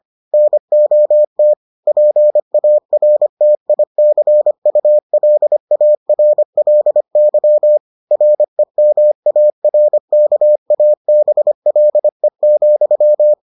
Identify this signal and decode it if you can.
Morse code, heard as a keyed tone, about 25 wpm, 600 Hz. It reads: NOT PARTICULARLY REMARKABLE,